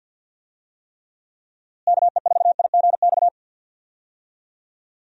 Morse code, heard as a keyed tone, about 46 wpm, 700 Hz. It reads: KE4IGX
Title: USA Call Signs - 6 Characters - Repeated 3x - 50wpm